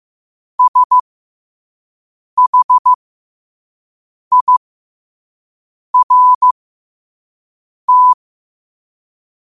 Morse code, heard as a keyed tone, about 15 wpm, spaced as for 5 wpm, 1000 Hz. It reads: SHIRT